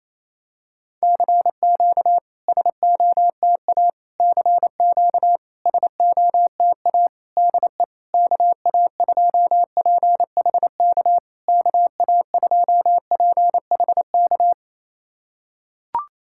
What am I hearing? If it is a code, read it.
Morse code, heard as a keyed tone, about 28 wpm, 700 Hz. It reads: CQ SOTA CQ SOTA DE KA2PHK KA2PHK
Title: SOTA QSO Practice 28wpm